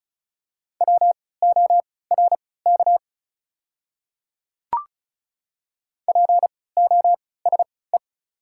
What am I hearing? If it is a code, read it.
Morse code, heard as a keyed tone, about 34 wpm, spaced as for 20 wpm, 700 Hz. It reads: WORK POSE
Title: 3x Character Spacing - Top 500 Words - 35wpm